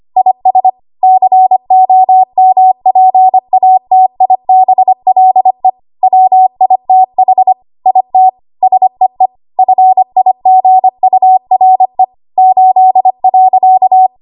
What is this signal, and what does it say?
Morse code, heard as a keyed tone, about 25 wpm, 750 Hz. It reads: IS COMPATIBLE WITH IT SEE FIGURE 8.
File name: Archive / 25 WPM / 25WPM